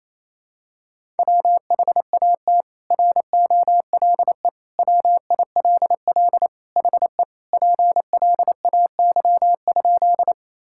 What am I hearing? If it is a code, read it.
Morse code, heard as a keyed tone, about 28 wpm, 700 Hz. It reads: WHAT ROLE WILL HE PLAY?
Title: Sentences from Top 1000 Words - Review Part 6 - 28wpm